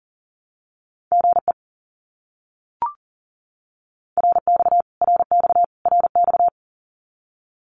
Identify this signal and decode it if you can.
Morse code, heard as a keyed tone, about 40 wpm, 700 Hz. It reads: GE RX RX RX
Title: Abbreviations Repeated 3x - 40wpm